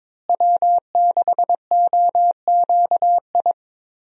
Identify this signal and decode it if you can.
Morse code, heard as a keyed tone, about 22 wpm, 700 Hz. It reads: W6OQI
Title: USA Call Signs - 5 Characters - Repeated 3x - 22wpm